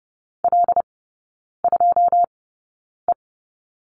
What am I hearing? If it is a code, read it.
Morse code, heard as a keyed tone, about 30 wpm, spaced as for 9 wpm, 700 Hz. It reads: L2E
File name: Lesson 16e - Letter c - Three Characters 30wpm